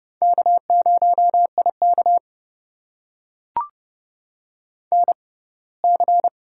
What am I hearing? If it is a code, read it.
Morse code, heard as a keyed tone, about 30 wpm, spaced as for 20 wpm, 700 Hz. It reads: K0IK NC
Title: USA Call Signs - 4 Characters - 6x Character Spacing - 30wpm